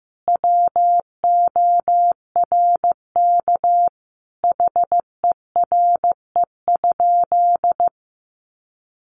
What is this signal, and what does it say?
Morse code, heard as a keyed tone, about 15 wpm, 700 Hz. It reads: WORK HERE?
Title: Sentences from Top 1200 Words 15wpm